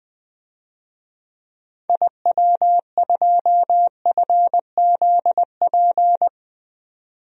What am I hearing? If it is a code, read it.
Morse code, heard as a keyed tone, about 20 wpm, 700 Hz. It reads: IW2FZP
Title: DXCC-100 Call Signs - 6 Characters - 20wpm